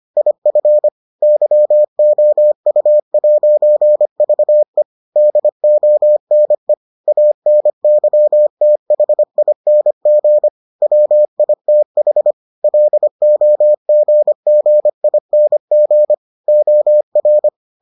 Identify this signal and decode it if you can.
Morse code, heard as a keyed tone, about 25 wpm, 600 Hz. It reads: IF YOU'VE DONE ANYTHING WITH LOGGING OR